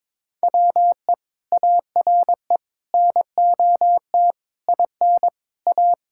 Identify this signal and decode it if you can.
Morse code, heard as a keyed tone, about 21 wpm, 700 Hz. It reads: WE ARE NOT IN A
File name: Sentences with 7-Word Phrases - 22wpm